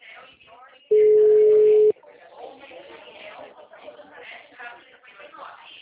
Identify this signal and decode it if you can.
Ringback tone (Europe, 425 Hz), with speech.